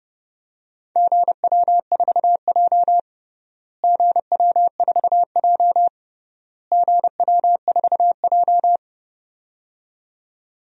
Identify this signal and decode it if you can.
Morse code, heard as a keyed tone, about 30 wpm, 700 Hz. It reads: GW4J GW4J GW4J
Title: DXCC-100 Call Signs - 4 Characters - Repeated 3x - 30wpm